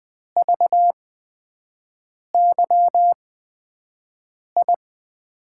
Morse code, heard as a keyed tone, about 20 wpm, spaced as for 5 wpm, 700 Hz. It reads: VYI
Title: USA Call Signs - 6 Characters - 8x Character Spacing - 20wpm